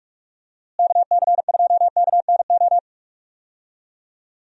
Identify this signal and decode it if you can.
Morse code, heard as a keyed tone, about 46 wpm, 700 Hz. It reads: KC2KNO
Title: USA Call Signs - 6 Characters - Repeated 2x - 45wpm